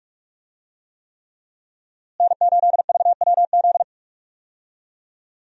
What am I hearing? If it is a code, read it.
Morse code, heard as a keyed tone, about 41 wpm, 700 Hz. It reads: N8VWZ
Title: USA Call Signs - 5 Characters - Repeated 3x - 45wpm